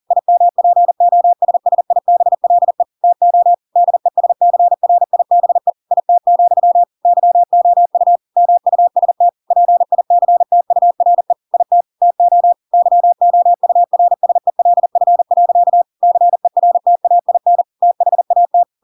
Morse code, heard as a keyed tone, about 40 wpm, 700 Hz. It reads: IMPOSSIBLE TO DESCRIBE IT, YOU MUST PICTURE IT TO YOURSELF. CERTAIN THAT
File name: Pinocchio / Pinocchio ch5 40 40 ik1hge